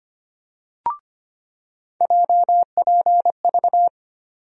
Morse code, heard as a keyed tone, about 25 wpm, 700 Hz. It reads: JPV
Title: USA Call Sign Postfixes - 3 Letters 25wpm